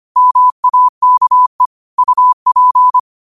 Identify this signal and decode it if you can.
Morse code, heard as a keyed tone, about 25 wpm, 1000 Hz. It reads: MAKE UP
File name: qotdfiles / 25wpm